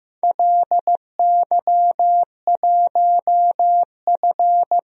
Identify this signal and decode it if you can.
Morse code, heard as a keyed tone, about 15 wpm, 700 Hz. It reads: LY1F
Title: DXCC-100 Call Signs - 4 Characters - 15wpm